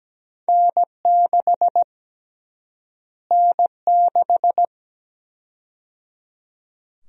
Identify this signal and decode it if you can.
Morse code, heard as a keyed tone, about 17 wpm, 700 Hz. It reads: N6 N6
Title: USA Call Sign Prefixes - Repeated 2x - 17wpm